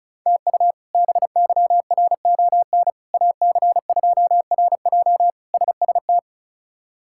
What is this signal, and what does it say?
Morse code, heard as a keyed tone, about 36 wpm, 700 Hz. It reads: TU BYRON AC2RJ SST
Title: K1USN SST QSO Practice - 35wpm